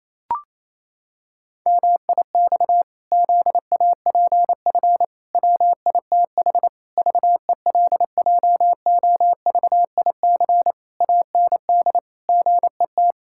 Morse code, heard as a keyed tone, about 27 wpm, 700 Hz. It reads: MIX ZAPF WITH VELJOVIC AND GET
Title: Pangram Challenge v2 - 28wpm